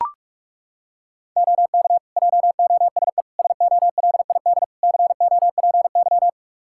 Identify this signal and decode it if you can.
Morse code, heard as a keyed tone, about 42 wpm, 700 Hz. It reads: OK JOSE SOLID COPY
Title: K1USN SST QSO Practice 45wpm